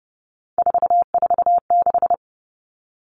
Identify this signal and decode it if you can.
Morse code, heard as a keyed tone, about 30 wpm, 700 Hz. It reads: 446
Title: Numbers - 3-Digit 30wpm